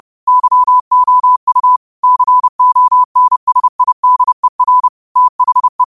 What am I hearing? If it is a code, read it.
Morse code, heard as a keyed tone, about 30 wpm, 1000 Hz. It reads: YOU CONSIDER THE
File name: qotdfiles / 30wpm